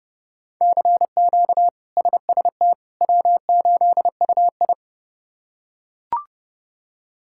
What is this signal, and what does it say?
Morse code, heard as a keyed tone, about 29 wpm, 700 Hz. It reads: CQ SST W8UI
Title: K1USN SST QSO Practice - 30wpm